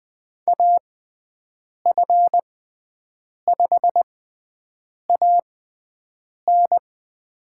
Morse code, heard as a keyed tone, about 20 wpm, spaced as for 7 wpm, 700 Hz. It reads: AF5AN
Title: USA Call Signs - 5 Characters - 6x Character Spacing - 20wpm